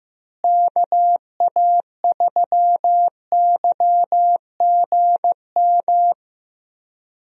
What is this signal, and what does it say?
Morse code, heard as a keyed tone, about 15 wpm, 700 Hz. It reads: KA3YGM
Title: USA Call Signs - 6 Characters - Repeated 3x - 15wpm